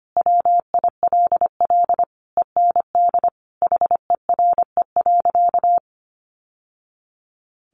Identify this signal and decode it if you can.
Morse code, heard as a keyed tone, about 25 wpm, 700 Hz. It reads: WILL END HERE.
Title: Sentences from Top 1200 Words 25wpm